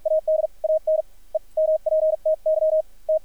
Morse code, heard as a keyed tone, about 53 wpm, spaced as for 41 wpm, 590 Hz, with radio noise. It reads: AD AN EMPTY T
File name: archived / CRSnet